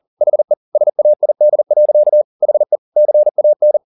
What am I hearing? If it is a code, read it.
Morse code, heard as a keyed tone, about 40 wpm, 600 Hz. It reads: HE SAID. HE CAN
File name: the old man and the sea 2 40wpm